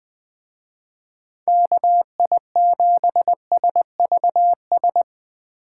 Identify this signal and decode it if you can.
Morse code, heard as a keyed tone, about 23 wpm, 700 Hz. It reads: KI7SVS
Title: Speed Racing - USA 2x3 Call Signs 20wpm